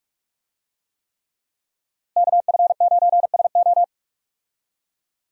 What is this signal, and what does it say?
Morse code, heard as a keyed tone, about 44 wpm, 700 Hz. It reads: KF9SO
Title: USA Call Signs - 5 Characters - Repeated 3x - 45wpm